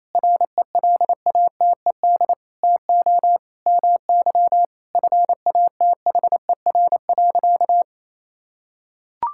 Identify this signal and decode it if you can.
Morse code, heard as a keyed tone, about 28 wpm, 700 Hz. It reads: RELATED TO MY FATHER.
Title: Sentences from Top 1000 Words - Review Part 5 - 28wpm